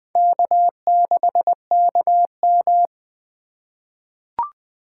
Morse code, heard as a keyed tone, about 20 wpm, 700 Hz. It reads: K6KM
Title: 2023 SOTA Chaser Call Signs - 20wpm